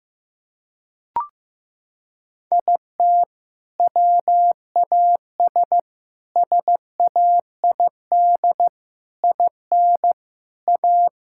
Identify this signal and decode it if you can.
Morse code, heard as a keyed tone, about 15 wpm, 700 Hz. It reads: IT WAS SAID IN A
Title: Sentences from Top 1600 Words 15wpm